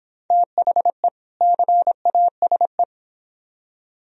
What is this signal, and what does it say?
Morse code, heard as a keyed tone, about 25 wpm, 700 Hz. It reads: THE CASE